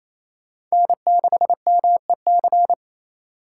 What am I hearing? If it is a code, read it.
Morse code, heard as a keyed tone, about 35 wpm, 700 Hz. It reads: N6MEC